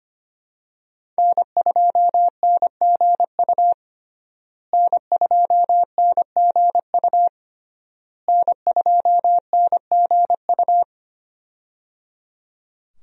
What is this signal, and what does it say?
Morse code, heard as a keyed tone, about 25 wpm, 700 Hz. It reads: N2NGU N2NGU N2NGU
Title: USA Call Signs - 5 Characters - Repeated 3x - 25wpm